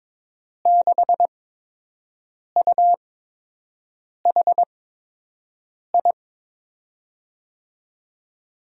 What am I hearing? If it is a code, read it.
Morse code, heard as a keyed tone, about 22 wpm, spaced as for 6 wpm, 700 Hz. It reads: 6UHI
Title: USA Call Signs - 5 Characters - 8x Character Spacing - 22wpm